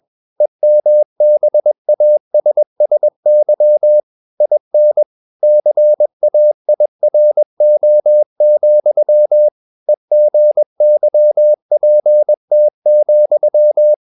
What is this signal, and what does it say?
Morse code, heard as a keyed tone, about 21 wpm, 600 Hz. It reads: EMBASSY IN CAIRO, EGYPT,
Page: CW med Gnister